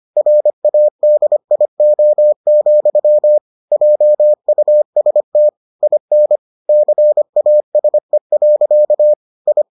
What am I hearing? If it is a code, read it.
Morse code, heard as a keyed tone, about 25 wpm, 600 Hz. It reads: RADIO, JUST IN CASE. I